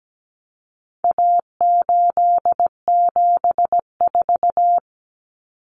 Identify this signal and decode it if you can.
Morse code, heard as a keyed tone, about 17 wpm, 700 Hz. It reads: A874
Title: Common Cut Numbers - 4-Digit 17wpm